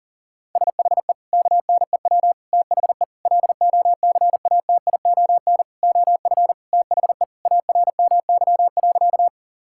Morse code, heard as a keyed tone, about 37 wpm, 700 Hz. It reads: SHE KNEW THE LOCATION OF THE ARMY.
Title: Sentences from Top 1000 Words - Review Part 6 - 40wpm